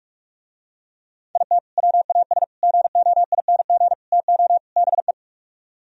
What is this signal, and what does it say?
Morse code, heard as a keyed tone, about 47 wpm, 700 Hz. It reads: IT WAS GOING TO BE